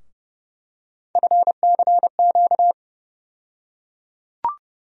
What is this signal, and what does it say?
Morse code, heard as a keyed tone, about 30 wpm, 700 Hz. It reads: FCQ
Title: USA Call Sign Postfixes - 3 Letters 30wpm